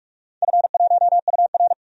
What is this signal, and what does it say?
Morse code, heard as a keyed tone, about 42 wpm, 700 Hz. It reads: F1UR
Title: DXCC-100 Call Signs - 4 Characters - 45wpm